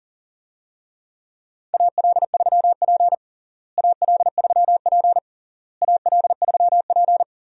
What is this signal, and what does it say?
Morse code, heard as a keyed tone, about 40 wpm, 700 Hz. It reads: AL3P AL3P AL3P
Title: USA Call Signs - 4 Characters - Repeated 3x - 40wpm